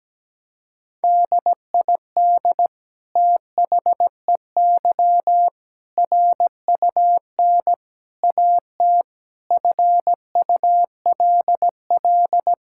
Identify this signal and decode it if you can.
Morse code, heard as a keyed tone, about 17 wpm, 700 Hz. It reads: DID THEY RUN AT FULL
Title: Sentences from Top 1000 Words - Review Part 5 - 17wpm